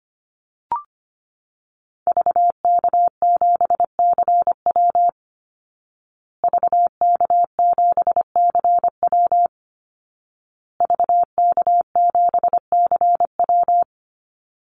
Morse code, heard as a keyed tone, about 25 wpm, 700 Hz. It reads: VK7CW VK7CW VK7CW
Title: Call Signs Repeated 2x - 25wpm